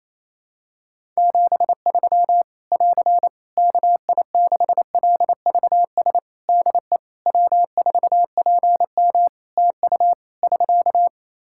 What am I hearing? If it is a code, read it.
Morse code, heard as a keyed tone, about 28 wpm, 700 Hz. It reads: 73 <AR> KI6LVS DE W4PM TU <SK>
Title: Practice QSO - Exchange 4 - Ending - 28wpm